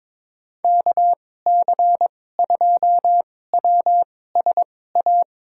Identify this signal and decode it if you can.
Morse code, heard as a keyed tone, about 22 wpm, spaced as for 16 wpm, 700 Hz. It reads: KC2WSA
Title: USA Call Signs - 6 Characters - 2x Character Spacing - 22wpm